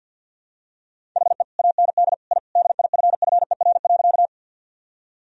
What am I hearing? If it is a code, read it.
Morse code, heard as a keyed tone, about 52 wpm, 700 Hz. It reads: HE AND I DIFFER.
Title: Sentences from Top 1000 Words - Review Part 1 - 50wpm